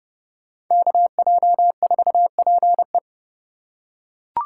Morse code, heard as a keyed tone, about 29 wpm, 700 Hz. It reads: KJ4PE
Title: USA Call Signs - 5 Characters - 4x Character Spacing - 30wpm